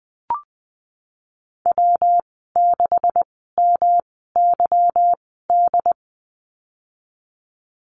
Morse code, heard as a keyed tone, about 20 wpm, spaced as for 15 wpm, 700 Hz. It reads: W6MYD
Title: 179 Lesson 29g - Slash - Call Signs 20wpm